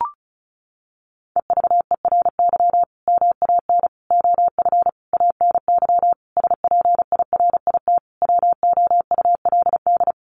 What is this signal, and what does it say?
Morse code, heard as a keyed tone, about 34 wpm, 700 Hz. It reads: EVERY MAN OF ANY SPIRIT WOULD
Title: Sentences from Top 1400 Words 35wpm